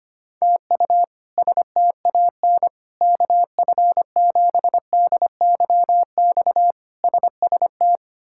Morse code, heard as a keyed tone, about 25 wpm, 700 Hz. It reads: TU STAN KF7DYX SST
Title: K1USN SST QSO Practice - 25wpm